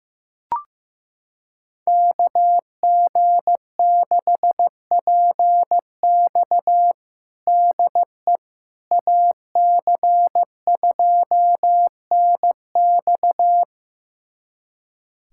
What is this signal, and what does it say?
Morse code, heard as a keyed tone, about 15 wpm, 700 Hz. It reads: KG6PX DE AC2NX
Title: USA Callsign DE Callsign - 5 Characters - 15wpm